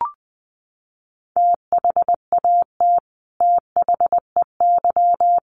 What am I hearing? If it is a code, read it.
Morse code, heard as a keyed tone, about 20 wpm, 700 Hz. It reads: THAT THEY
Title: Sets of 5 Words Encore 20wpm